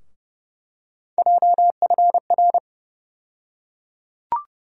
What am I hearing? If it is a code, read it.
Morse code, heard as a keyed tone, about 32 wpm, 700 Hz. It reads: JFR
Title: USA Call Sign Postfixes - 3 Letters 30wpm